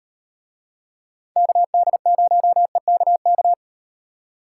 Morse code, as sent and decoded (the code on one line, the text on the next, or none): -.- -.. ----- . -.- -.-
KD0EKK